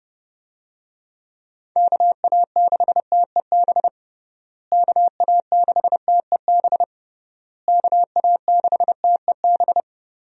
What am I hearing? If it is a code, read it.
Morse code, heard as a keyed tone, about 32 wpm, 700 Hz. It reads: KA6TEB KA6TEB KA6TEB
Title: USA Call Signs - 6 Characters - Repeated 3x - 30wpm